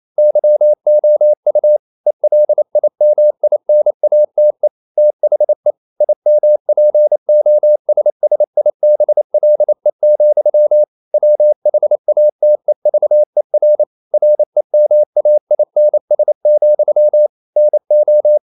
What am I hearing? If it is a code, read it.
Morse code, heard as a keyed tone, about 28 wpm, 600 Hz. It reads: YOU ELIMINATE THE IMPOSSIBLE, WHATEVER REMAINS, NO
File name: CW / Practice / EnglishQuotations0228